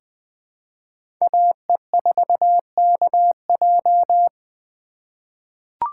Morse code, heard as a keyed tone, about 20 wpm, 700 Hz. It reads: AE4KJ
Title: USA Call Signs - 5 Characters - 3x Character Spacing - 20wpm